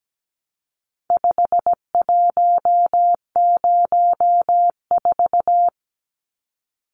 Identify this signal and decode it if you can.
Morse code, heard as a keyed tone, about 17 wpm, 700 Hz. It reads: 5104
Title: Numbers - 4-Digit 17wpm